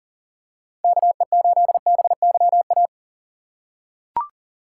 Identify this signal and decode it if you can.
Morse code, heard as a keyed tone, about 40 wpm, 700 Hz. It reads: KE8BYA
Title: USA Call Signs - 6 Characters - 3x Character Spacing - 40wpm